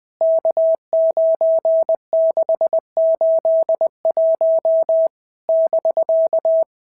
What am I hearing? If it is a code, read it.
Morse code, heard as a keyed tone, about 20 wpm, 650 Hz. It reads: K9681 <BK>
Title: POTA QSO Practice - 20wpm